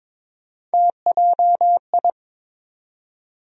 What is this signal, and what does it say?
Morse code, heard as a keyed tone, about 22 wpm, 700 Hz. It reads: TJI
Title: USA Call Sign Postfixes - 3 Letters 22wpm